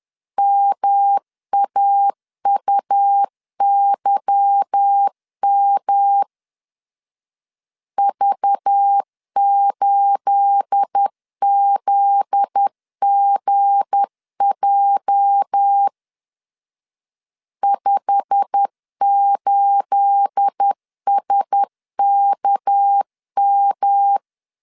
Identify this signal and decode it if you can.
Morse code, heard as a keyed tone, about 10 wpm, 800 Hz. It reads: MAUYM V8ZGJ 58SKM
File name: CW / DET / 1 9 1500C S B 10wpm 4439